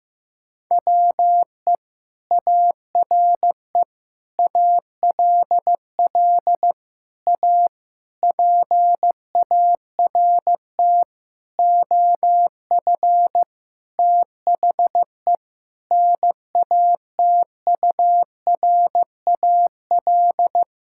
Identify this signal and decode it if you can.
Morse code, heard as a keyed tone, about 16 wpm, 700 Hz. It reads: WE ARE ALL A PART OF THE NATURAL
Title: Sentences from Top 1000 Words - Review Part 6 - 15wpm